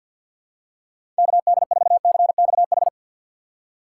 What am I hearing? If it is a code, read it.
Morse code, heard as a keyed tone, about 52 wpm, 700 Hz. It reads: KD4CXH
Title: USA Call Signs - 6 Characters - 8x Character Spacing - 50wpm